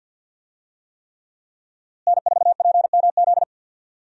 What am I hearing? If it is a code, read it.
Morse code, heard as a keyed tone, about 49 wpm, 700 Hz. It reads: N4PMZ
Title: USA Call Signs - 5 Characters - Repeated 3x - 50wpm